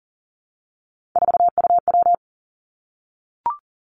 Morse code, heard as a keyed tone, about 37 wpm, 700 Hz. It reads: 4UW
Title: Three Challenging Letter-Number Combinations 40wpm